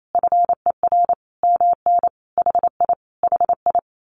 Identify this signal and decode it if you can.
Morse code, heard as a keyed tone, about 28 wpm, 700 Hz. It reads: FER MN HI HI
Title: Common QSO Phrases Encore 28wpm